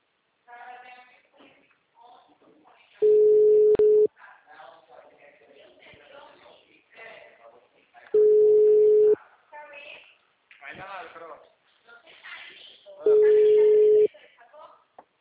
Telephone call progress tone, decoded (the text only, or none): ringback tone